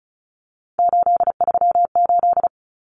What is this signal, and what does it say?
Morse code, heard as a keyed tone, about 34 wpm, 700 Hz. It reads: 838
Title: Numbers - 3-Digit 35wpm